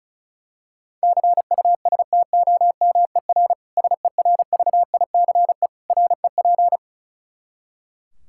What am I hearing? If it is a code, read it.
Morse code, heard as a keyed tone, about 36 wpm, 700 Hz. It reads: CUSTOMER SERVICE REP
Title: K1USN SST QSO Practice 35wpm